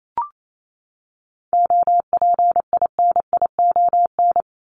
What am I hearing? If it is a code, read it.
Morse code, heard as a keyed tone, about 28 wpm, 700 Hz. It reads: OPINION
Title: Top 1001-1250 Words 28wpm